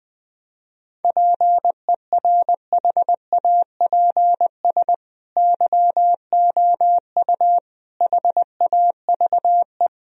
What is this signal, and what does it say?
Morse code, heard as a keyed tone, about 20 wpm, 700 Hz. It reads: PERHAPS YOU HAVE
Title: Sentences from Top 1000 Words - Review Part 1 - 20wpm